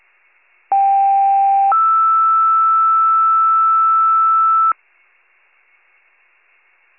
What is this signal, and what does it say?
Two-tone page, heard as a one-second tone, then a three-second tone, about 790 then 1320 Hz. It is heard from 0.7 to 4.7 s.
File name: pager / UNKNOWN / wcalert